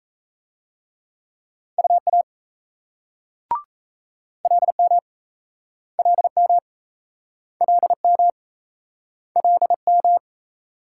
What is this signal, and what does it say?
Morse code, heard as a keyed tone, about 35 wpm, 700 Hz. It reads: UA LM LM LM LM